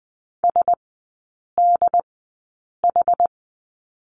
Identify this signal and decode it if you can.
Morse code, heard as a keyed tone, about 20 wpm, spaced as for 8 wpm, 700 Hz. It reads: SDH
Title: 107 Lesson 18e - Letter m - Three Characters 20wpm